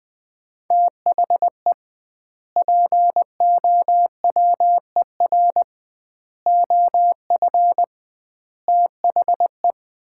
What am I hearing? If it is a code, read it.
Morse code, heard as a keyed tone, about 20 wpm, 700 Hz. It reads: THE POWER OF THE
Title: 2x Word-Spacing - Sets of 4 Words - 20wpm